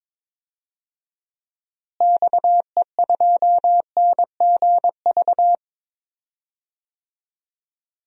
Morse code, heard as a keyed tone, about 22 wpm, 700 Hz. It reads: XE2NGV
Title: DXCC-100 Call Signs - 6 Characters - Repeated 3x - 22wpm